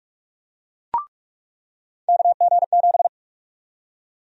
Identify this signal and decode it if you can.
Morse code, heard as a keyed tone, about 41 wpm, 700 Hz. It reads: KG7
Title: USA Call Sign Prefixes - 45wpm